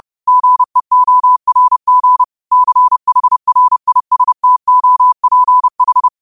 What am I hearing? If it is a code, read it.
Morse code, heard as a keyed tone, about 30 wpm, 1000 Hz. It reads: GEORG CHRISTOPH
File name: qotdfiles / 30wpm